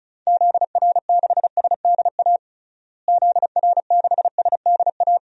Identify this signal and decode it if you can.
Morse code, heard as a keyed tone, about 34 wpm, 700 Hz. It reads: ZR6SDA ZR6SDA
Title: DXCC-100 Call Signs - 6 Characters - Repeated 3x - 35wpm